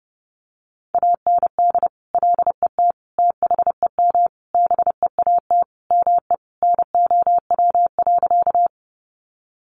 Morse code, heard as a keyed tone, about 30 wpm, 700 Hz. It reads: AND LET THEM BEAT ME NOW.
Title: Sentences from Top 1300 Words 30wpm